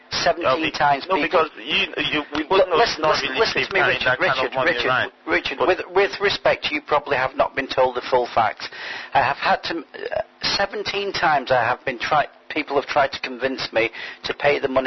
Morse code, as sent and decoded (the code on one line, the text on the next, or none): none